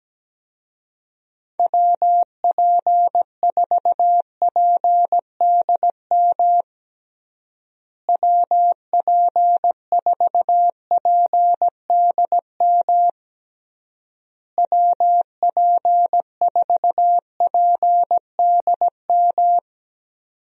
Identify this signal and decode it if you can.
Morse code, heard as a keyed tone, about 17 wpm, 700 Hz. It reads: WP4PDM WP4PDM WP4PDM
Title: USA Call Signs - 6 Characters - Repeated 3x - 17wpm